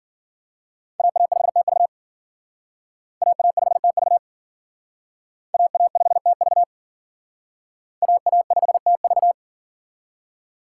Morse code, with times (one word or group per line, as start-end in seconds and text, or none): 1.0-1.9 AA5TV
3.2-4.2 AA5TV
5.5-6.6 AA5TV
8.0-9.3 AA5TV